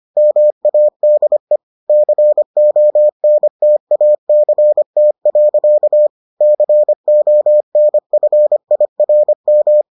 Morse code, heard as a keyed tone, about 25 wpm, 600 Hz. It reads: MADE CONTACT. CONFIRM